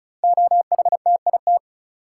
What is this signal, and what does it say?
Morse code, heard as a keyed tone, about 36 wpm, 700 Hz. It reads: OHTIT
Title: Code Groups - 5 Letters - 35wpm